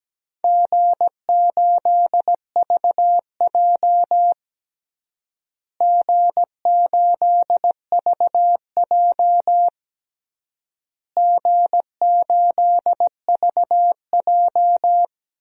Morse code, heard as a keyed tone, about 17 wpm, 700 Hz. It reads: G8VJ G8VJ G8VJ
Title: DXCC-100 Call Signs - 4 Characters - Repeated 3x - 17wpm